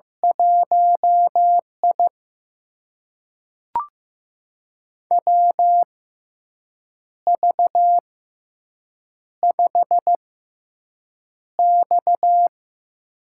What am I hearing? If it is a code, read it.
Morse code, heard as a keyed tone, about 15 wpm, spaced as for 6 wpm, 700 Hz. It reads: 1I WV5X
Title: USA Call Signs - 6x Character Spacing - 15wpm